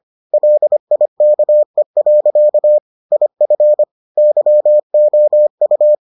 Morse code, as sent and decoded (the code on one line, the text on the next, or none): .-.. .. -.- . .-.-.- / .. ..-. / -.-- --- ..-
LIKE. IF YOU